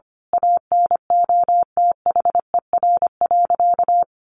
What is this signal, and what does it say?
Morse code, heard as a keyed tone, about 25 wpm, 700 Hz. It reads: ANOTHER.
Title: Sentences from Top 1400 Words 25wpm